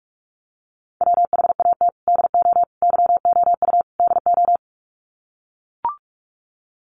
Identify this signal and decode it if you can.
Morse code, heard as a keyed tone, about 44 wpm, 700 Hz. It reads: WHAT DO YOU DO
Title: Sets of 4 Words 45wpm